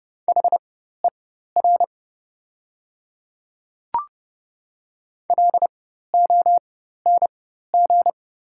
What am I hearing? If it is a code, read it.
Morse code, heard as a keyed tone, about 30 wpm, spaced as for 14 wpm, 700 Hz. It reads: HER LONG
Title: 4x Character Spacing - Top 200 Words - 30wpm